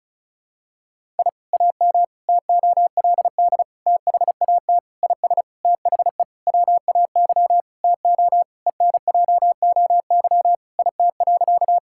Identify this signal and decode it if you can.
Morse code, heard as a keyed tone, about 36 wpm, 700 Hz. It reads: I AM TOLD THAT IS THE WAY TO ENJOY IT.
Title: Sentences from Top 1000 Words - Review Part 5 - 35wpm